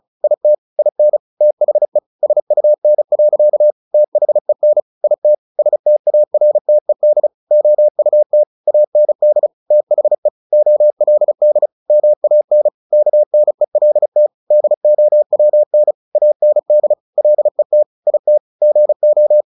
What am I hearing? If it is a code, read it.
Morse code, heard as a keyed tone, about 34 wpm, 600 Hz. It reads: IT IN THE SUN. THEN IT STARTED OUT AND THE OLD MAN KNELT DOWN AND LET IT GO
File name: the old man and the sea 2 35wpm